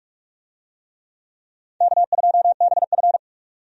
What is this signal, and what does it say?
Morse code, heard as a keyed tone, about 46 wpm, 700 Hz. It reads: K2BF
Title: USA Call Signs - 4 Characters - 45wpm